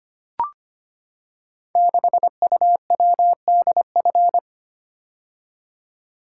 Morse code, heard as a keyed tone, about 25 wpm, 700 Hz. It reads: 6UWDF
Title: Code Groups - 5 Letter:Numbers - 25wpm